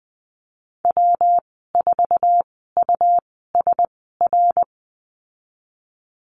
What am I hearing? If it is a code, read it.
Morse code, heard as a keyed tone, about 20 wpm, spaced as for 14 wpm, 700 Hz. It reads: W4USR
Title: 147 Lesson 24g - Letter y - Call Signs 20wpm